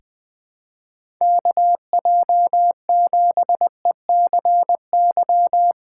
Morse code, heard as a keyed tone, about 20 wpm, 700 Hz. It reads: KJ7ECY
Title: USA Call Signs - 6 Characters - 2x Character Spacing - 20wpm